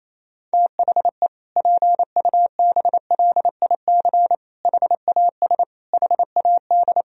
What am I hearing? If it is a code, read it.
Morse code, heard as a keyed tone, about 28 wpm, 700 Hz. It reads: THE PUBLIC HAS HAD
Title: Sentences from Top 1000 Words - Review Part 3 - 28wpm